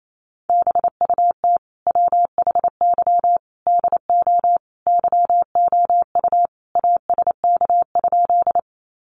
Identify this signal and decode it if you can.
Morse code, heard as a keyed tone, about 28 wpm, 700 Hz. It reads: BUT WHY DO YOU ASK?
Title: Sentences from Top 200 Words 28wpm